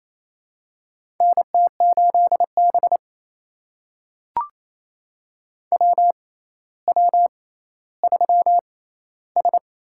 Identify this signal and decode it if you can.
Morse code, heard as a keyed tone, about 28 wpm, spaced as for 14 wpm, 700 Hz. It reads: NT8B WW3S
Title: USA Call Signs - 4 Characters - 6x Character Spacing - 28wpm